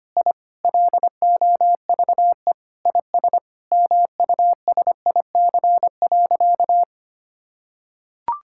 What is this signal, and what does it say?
Morse code, heard as a keyed tone, about 25 wpm, 700 Hz. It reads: I LOVE IS MUSIC.
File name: practice / 25wpm / Sentences with 4-Word Phrases - 25wpm